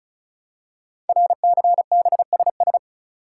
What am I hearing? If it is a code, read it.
Morse code, heard as a keyed tone, about 36 wpm, 700 Hz. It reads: RCBSS
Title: Code Groups - 5 Letters - 35wpm